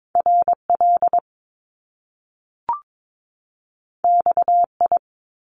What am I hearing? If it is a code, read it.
Morse code, heard as a keyed tone, about 22 wpm, 700 Hz. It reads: RL XI
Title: Top Two Letter Combinations in Words 22wpm